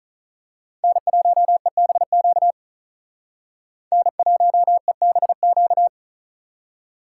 Morse code, heard as a keyed tone, about 39 wpm, 700 Hz. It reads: N1EBQ N1EBQ
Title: Speed Racing - USA 1x3 Call Signs 35wpm